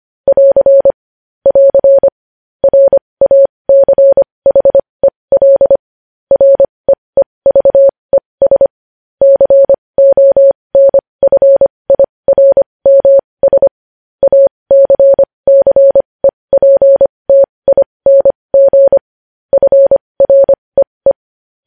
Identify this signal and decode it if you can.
Morse code, heard as a keyed tone, about 25 wpm, spaced as for 20 wpm, 550 Hz. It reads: <AR> <AR> RACHEL REEVES CONFIRMS ACCEPTING FREE
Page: news-headlines-in-morse-code-at-20-wpm